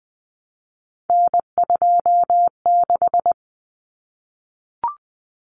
Morse code, heard as a keyed tone, about 20 wpm, 700 Hz. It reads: N26
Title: Common Cut Numbers - 3-Digit 20wpm